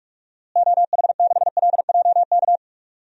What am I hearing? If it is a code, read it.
Morse code, heard as a keyed tone, about 46 wpm, 690 Hz. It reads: OH6LJK
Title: DXCC-100 Call Signs - 6 Characters - Repeated 2x - 45wpm